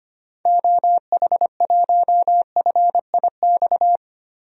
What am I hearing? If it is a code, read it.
Morse code, heard as a keyed tone, about 25 wpm, 700 Hz. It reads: OH1FIX
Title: DXCC-100 Call Signs - 6 Characters - 25wpm